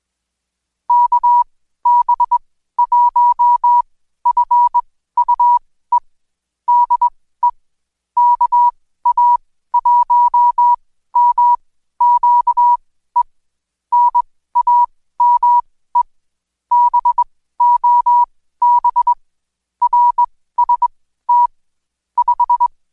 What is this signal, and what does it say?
Morse code, heard as a keyed tone, about 20 wpm, spaced as for 14 wpm, 970 Hz. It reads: KB1FUE DE KA1MQE NAME BOB RST 5
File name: Code Quick / AudioDownld2 / Seg123-Day28-DayDrill 28